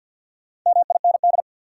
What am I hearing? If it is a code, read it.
Morse code, heard as a keyed tone, about 51 wpm, 700 Hz. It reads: MIND
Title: Sets of 5 Words - New Word Drill - 50wpm